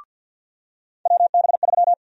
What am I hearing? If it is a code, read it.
Morse code, heard as a keyed tone, about 47 wpm, 700 Hz. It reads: WB3
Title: USA Call Sign Prefixes - 50wpm